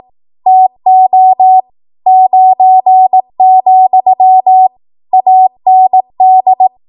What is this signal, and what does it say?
Morse code, heard as a keyed tone, about 18 wpm, 750 Hz. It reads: TO 9, AND
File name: Archive / 18 WPM / 18WPM